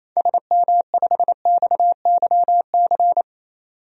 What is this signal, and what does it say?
Morse code, heard as a keyed tone, about 28 wpm, 700 Hz. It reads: SM5XYC
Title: DXCC-100 Call Signs - 6 Characters - 28wpm